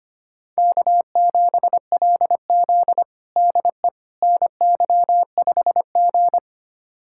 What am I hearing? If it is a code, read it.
Morse code, heard as a keyed tone, about 25 wpm, 700 Hz. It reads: K7LZ DE NY5G
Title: USA Callsign DE Callsign - 4 Characters - 25wpm